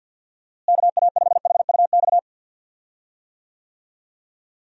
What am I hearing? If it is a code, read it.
Morse code, heard as a keyed tone, about 47 wpm, 700 Hz. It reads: KA5HUX
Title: USA Call Signs - 6 Characters - Repeated 2x - 50wpm